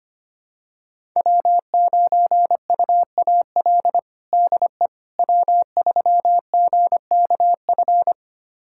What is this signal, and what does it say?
Morse code, heard as a keyed tone, about 25 wpm, 700 Hz. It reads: W9UAL DE W3GKF
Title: USA Callsign DE Callsign - 5 Characters - 25wpm